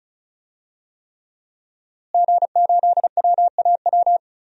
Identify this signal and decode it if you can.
Morse code, heard as a keyed tone, about 34 wpm, 690 Hz. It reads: G8WAW
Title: DXCC-100 Call Signs - 5 Characters - Repeated 3x - 35wpm